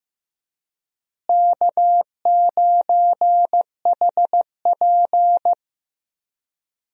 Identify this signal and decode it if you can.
Morse code, heard as a keyed tone, about 15 wpm, 700 Hz. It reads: K9HP
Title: USA Call Signs - 4 Characters - Repeated 2x - 15wpm